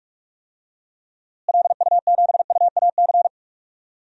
Morse code, heard as a keyed tone, about 46 wpm, 700 Hz. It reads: LU7UAC
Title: DXCC-100 Call Signs - 6 Characters - Repeated 3x - 45wpm